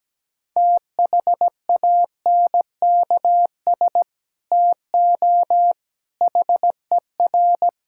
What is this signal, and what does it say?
Morse code, heard as a keyed tone, about 17 wpm, 700 Hz. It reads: THANKS TO HER
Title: Sentences from Top 1000 Words - Review Part 5 - 17wpm